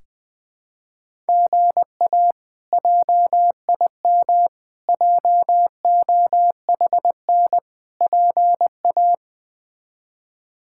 Morse code, heard as a keyed tone, about 20 wpm, 700 Hz. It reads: GA JIM JOHN PA
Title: K1USN SST QSO Practice - 20wpm